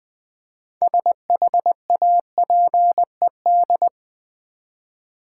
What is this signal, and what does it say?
Morse code, heard as a keyed tone, about 20 wpm, 700 Hz. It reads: SHAPED
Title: 6 Letter Words - First 3 Letters Given - 20wpm